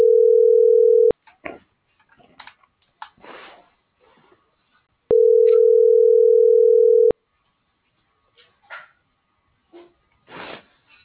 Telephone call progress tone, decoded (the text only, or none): ringback tone